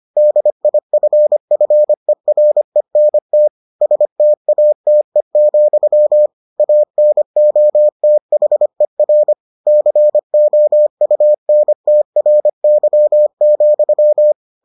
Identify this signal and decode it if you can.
Morse code, heard as a keyed tone, about 25 wpm, 600 Hz. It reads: DIFFERENT STATE, ANOTHER COUNTRY,